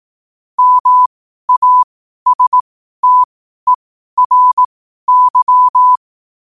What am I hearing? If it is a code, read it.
Morse code, heard as a keyed tone, about 18 wpm, spaced as for 12 wpm, 1000 Hz. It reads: MASTERY
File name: qotdfiles / 12wpm